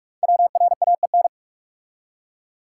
Morse code, heard as a keyed tone, about 44 wpm, 700 Hz. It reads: WRAEN
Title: Code Groups - 5 Letters - 45wpm